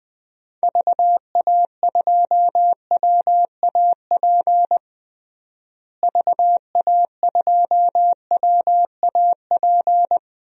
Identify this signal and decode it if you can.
Morse code, heard as a keyed tone, about 20 wpm, 700 Hz. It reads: VA2WAP VA2WAP
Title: DXCC-100 Call Signs - 6 Characters - Repeated 2x - 20wpm